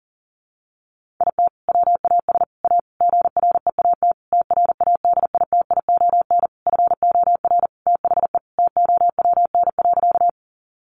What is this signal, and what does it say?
Morse code, heard as a keyed tone, about 40 wpm, 700 Hz. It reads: IT WAS A GREAT TRADITION FOR THE TOWN.
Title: Sentences from Top 1500 Words 40wpm